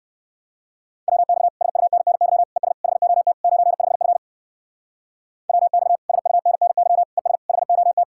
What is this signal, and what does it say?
Morse code, heard as a keyed tone, about 61 wpm, 700 Hz. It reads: WX SUNNY ES HOT 95F WX SUNNY ES HOT